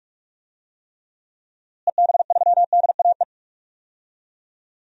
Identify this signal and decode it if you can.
Morse code, heard as a keyed tone, about 46 wpm, 700 Hz. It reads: EB3DAE
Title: DXCC-100 Call Signs - 6 Characters - Repeated 3x - 45wpm